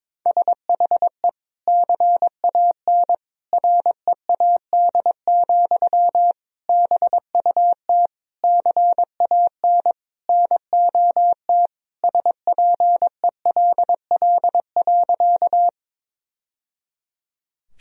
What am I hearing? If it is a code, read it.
Morse code, heard as a keyed tone, about 22 wpm, 700 Hz. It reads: SHE CAN READ, BUT CAN NOT SPELL.
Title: Sentences from Top 1000 Words - Review Part 1 - 22wpm